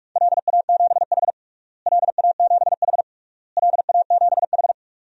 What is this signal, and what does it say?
Morse code, heard as a keyed tone, about 46 wpm, 700 Hz. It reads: LA7H LA7H LA7H
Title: DXCC-100 Call Signs - 4 Characters - Repeated 3x - 45wpm